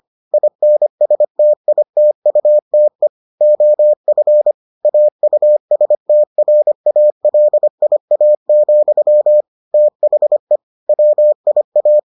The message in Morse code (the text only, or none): INSTITUTE OF AUSTRALIA, THE WIA